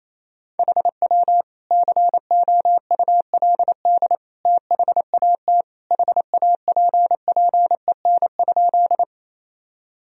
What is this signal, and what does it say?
Morse code, heard as a keyed tone, about 27 wpm, 700 Hz. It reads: HW COULD THAT HAPPEN?
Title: Sentences from Top 400 Words and Abbreviations 28wpm